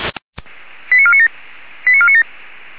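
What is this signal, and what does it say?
Five-tone selective call (ZVEI), heard as quick runs of five tones, tones of about 70 ms. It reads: 89497 89497
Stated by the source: Five-tone ZVEI selective call received by SDR, from feuerwehr-orlamuende.org fire brigade audio archive.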